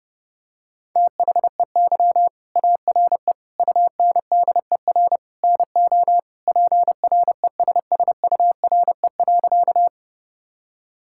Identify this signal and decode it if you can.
Morse code, heard as a keyed tone, about 30 wpm, 700 Hz. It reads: THEY ARE UNDER NO PRESSURE.
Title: Sentences from Top 1000 Words - Review Part 6 - 30wpm